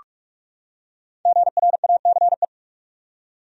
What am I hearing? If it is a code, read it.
Morse code, heard as a keyed tone, about 43 wpm, 700 Hz. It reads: GRACE
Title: USA Common Names - 45wpm